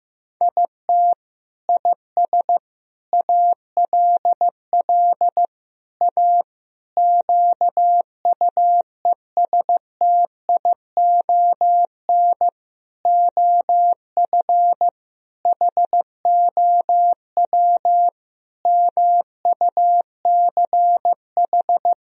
Morse code, heard as a keyed tone, about 15 wpm, 700 Hz. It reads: IT IS ALL A QUESTION OF HOW MUCH
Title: Sentences from Top 1000 Words - Review Part 3 - 15wpm